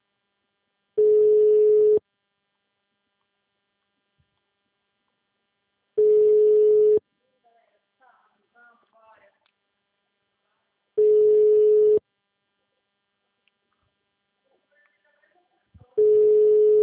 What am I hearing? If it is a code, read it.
Ringback tone (Europe, 425 Hz), with speech.
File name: out- -2024- - - .41435